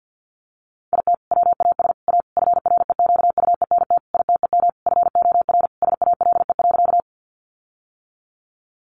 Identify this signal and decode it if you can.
Morse code, heard as a keyed tone, about 50 wpm, 700 Hz. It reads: IT WAS A FREQUENT ITEM FOR SALE.
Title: Sentences from Top 1400 Words 50wpm